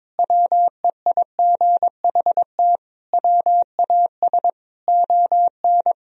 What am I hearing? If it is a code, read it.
Morse code, heard as a keyed tone, about 21 wpm, 700 Hz. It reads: WEIGHT WAS ON
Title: Sentences from Top 1000 Words - Review Part 3 - 22wpm